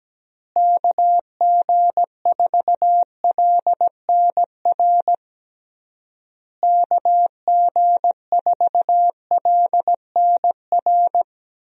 Morse code, heard as a keyed tone, about 17 wpm, 700 Hz. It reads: KG4LNR KG4LNR
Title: USA Call Signs - 6 Characters - Repeated 3x - 17wpm